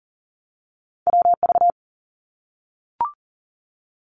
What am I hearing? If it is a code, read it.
Morse code, heard as a keyed tone, about 40 wpm, 700 Hz. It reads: WV
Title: US States Repeated 3x 40wpm